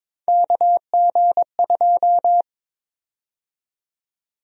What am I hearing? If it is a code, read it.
Morse code, heard as a keyed tone, about 22 wpm, 700 Hz. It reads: KG2
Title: USA Call Sign Prefixes - 22wpm